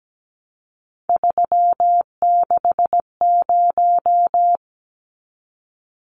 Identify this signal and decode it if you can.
Morse code, heard as a keyed tone, about 17 wpm, 700 Hz. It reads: 360
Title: Numbers - 3-Digit 17wpm